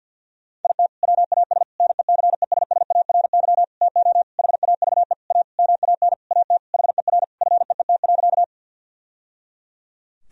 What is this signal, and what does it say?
Morse code, heard as a keyed tone, about 51 wpm, 700 Hz. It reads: IT WAS NECESSARY TO HAVE A MAN AT HER FEET.
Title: Sentences from Top 1000 Words - Review Part 1 - 50wpm